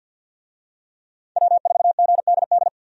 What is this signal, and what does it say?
Morse code, heard as a keyed tone, about 48 wpm, 700 Hz. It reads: W4GDD
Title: USA Call Signs - 5 Characters - 50wpm